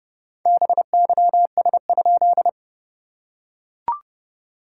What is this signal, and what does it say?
Morse code, heard as a keyed tone, about 30 wpm, 700 Hz. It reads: BYS?
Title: SOTA QSO Practice 30wpm